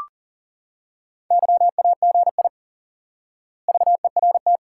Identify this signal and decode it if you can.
Morse code, heard as a keyed tone, about 38 wpm, 700 Hz. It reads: YAGI VERT
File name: IQE Course 3 - Lesson 89 - YAGI - Two Elements - 40wpm